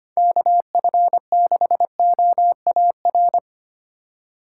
Morse code, heard as a keyed tone, about 30 wpm, 700 Hz. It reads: KF6OAR
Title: Speed Racing - USA 2x3 Call Signs 25wpm